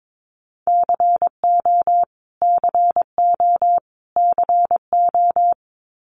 Morse code, heard as a keyed tone, about 22 wpm, 700 Hz. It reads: CO CO CO
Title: US States Repeated 3x 22wpm